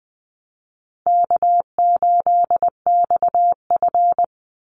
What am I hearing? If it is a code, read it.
Morse code, heard as a keyed tone, about 20 wpm, 700 Hz. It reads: K8XF
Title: Easy Call Signs Repeated 2x - 20wpm